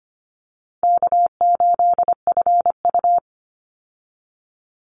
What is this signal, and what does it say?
Morse code, heard as a keyed tone, about 25 wpm, 700 Hz. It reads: K8FU
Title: Easy Call Signs Repeated 2x - 25wpm